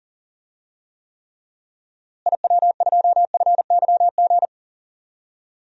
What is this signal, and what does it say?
Morse code, heard as a keyed tone, about 40 wpm, 700 Hz. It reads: IW2FYG
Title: DXCC-100 Call Signs - 6 Characters - Repeated 3x - 40wpm